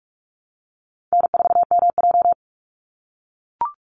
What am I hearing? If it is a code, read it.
Morse code, heard as a keyed tone, about 44 wpm, 690 Hz. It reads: N4MJ
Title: NA SOTA Chaser Callsigns 45wpm